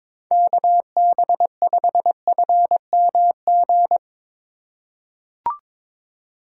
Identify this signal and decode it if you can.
Morse code, heard as a keyed tone, about 22 wpm, 700 Hz. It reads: KB5FMG
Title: USA Call Signs - 6 Characters - 4x Character Spacing - 22wpm